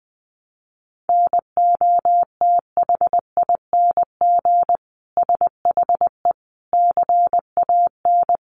Morse code, heard as a keyed tone, about 20 wpm, 700 Hz. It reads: NOTHING SHE CAN
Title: Sentences from Top 500 Words 20wpm